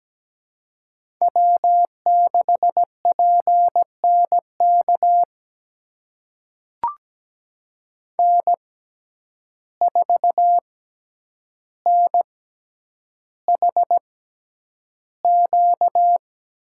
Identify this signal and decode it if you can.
Morse code, heard as a keyed tone, about 17 wpm, spaced as for 9 wpm, 700 Hz. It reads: W6PNK N4NHQ
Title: USA Call Signs - 5 Characters - 6x Character Spacing - 17wpm